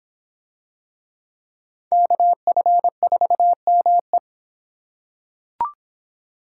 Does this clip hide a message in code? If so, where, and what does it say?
Yes: Morse code from 1.9 to 4.2 s, heard as a keyed tone, about 23 wpm, 700 Hz. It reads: KF4ME